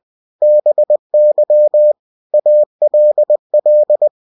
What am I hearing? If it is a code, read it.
Morse code, heard as a keyed tone, about 20 wpm, 600 Hz. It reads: BY ALL
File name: cw-books / winnie-the-pooh / winnie-20wpm-page-0047